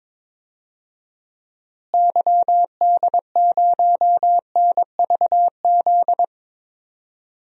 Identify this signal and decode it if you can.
Morse code, heard as a keyed tone, about 22 wpm, 700 Hz. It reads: YD0NVZ
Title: DXCC-100 Call Signs - 6 Characters - Repeated 3x - 22wpm